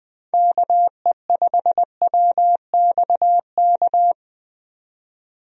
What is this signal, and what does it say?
Morse code, heard as a keyed tone, about 20 wpm, 700 Hz. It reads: KE5WXK
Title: USA Call Signs - 6 Characters - 20wpm